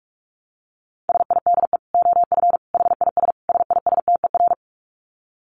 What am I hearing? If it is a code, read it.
Morse code, heard as a keyed tone, about 44 wpm, 700 Hz. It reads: SIDE OF HIS SISTER
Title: Dit Challenge 45wpm